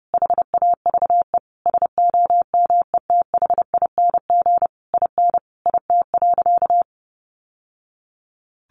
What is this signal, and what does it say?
Morse code, heard as a keyed tone, about 30 wpm, 700 Hz. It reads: HAVE SOMETHING IN IT.
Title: Sentences from Top 1100 Words 30wpm